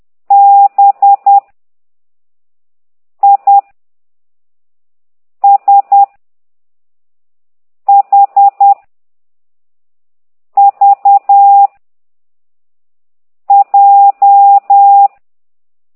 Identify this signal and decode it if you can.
Morse code, heard as a keyed tone, about 10 wpm, 800 Hz. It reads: B I S H V J
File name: ct2idw / exerc6